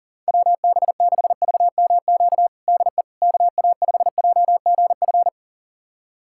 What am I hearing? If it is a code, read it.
Morse code, heard as a keyed tone, about 40 wpm, 700 Hz. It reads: WB6VMQ DE KA5JGF
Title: USA Callsign DE Callsign - 6 Characters - 40wpm